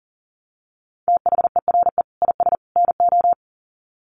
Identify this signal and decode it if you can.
Morse code, heard as a keyed tone, about 40 wpm, 700 Hz. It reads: THERE IS NO
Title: Sets of 3 Words 40wpm